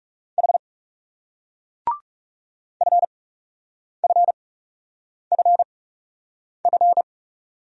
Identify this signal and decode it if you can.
Morse code, heard as a keyed tone, about 37 wpm, spaced as for 8 wpm, 700 Hz. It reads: H FFFF
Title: Speed-Racing Single Letters 30wpm